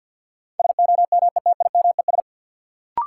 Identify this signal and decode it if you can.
Morse code, heard as a keyed tone, about 51 wpm, 700 Hz. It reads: SOMETIMES
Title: Sentences from Top 600 Words - New Word Drill - 50wpm